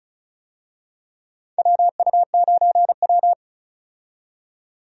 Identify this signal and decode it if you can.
Morse code, heard as a keyed tone, about 34 wpm, 700 Hz. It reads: WU9W
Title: USA Call Signs - 4 Characters - 35wpm